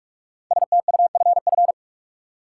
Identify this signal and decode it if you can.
Morse code, heard as a keyed tone, about 46 wpm, 700 Hz. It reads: STUFF
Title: Sentences from Top 1400 Words - New Word Drill - 45wpm